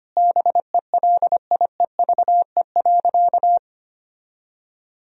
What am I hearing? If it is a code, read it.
Morse code, heard as a keyed tone, about 25 wpm, 700 Hz. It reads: BELIEVE.
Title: Sentences from Top 1000 Words - Review Part 4 - 25wpm